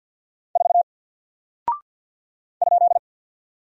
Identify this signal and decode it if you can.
Morse code, heard as a keyed tone, about 48 wpm, 700 Hz. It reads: 4 ?
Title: Single Character (Letter:Number:Punctuation:Prosign) 50wpm